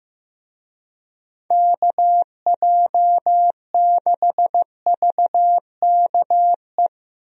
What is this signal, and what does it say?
Morse code, heard as a keyed tone, about 16 wpm, 700 Hz. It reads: KJ6VKE
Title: USA Call Signs - 6 Characters - Repeated 3x - 15wpm